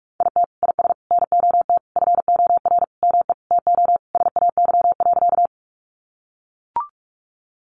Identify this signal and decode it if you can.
Morse code, heard as a keyed tone, about 44 wpm, 700 Hz. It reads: IT IS NOT FOR ME TO SAY.
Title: Sentences from Top 600 Words 45wpm